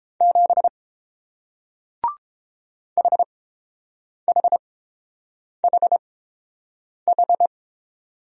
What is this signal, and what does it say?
Morse code, heard as a keyed tone, about 30 wpm, spaced as for 7 wpm, 700 Hz. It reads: 7 HHHH